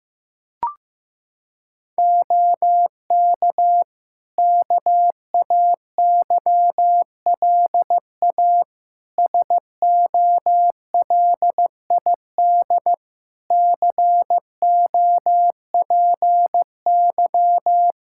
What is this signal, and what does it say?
Morse code, heard as a keyed tone, about 15 wpm, 700 Hz. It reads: OK KAYLA SOLID COPY
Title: K1USN SST QSO Practice 15wpm